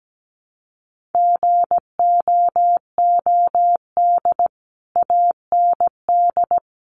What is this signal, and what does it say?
Morse code, heard as a keyed tone, about 17 wpm, 700 Hz. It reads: GOOD AND
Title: Sentences from Top 1200 Words 17wpm